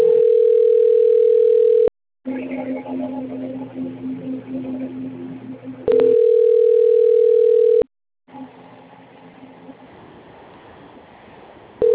Ringback tone (North America), 440 + 480 Hz.